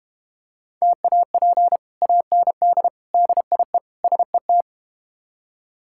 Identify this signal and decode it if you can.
Morse code, heard as a keyed tone, about 32 wpm, 700 Hz. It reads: TAP AND DIE SET